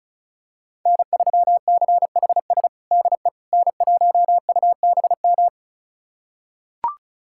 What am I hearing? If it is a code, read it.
Morse code, heard as a keyed tone, about 36 wpm, 700 Hz. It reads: N3CHS DE N1UBM
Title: USA Callsign DE Callsign - 5 Characters - 35wpm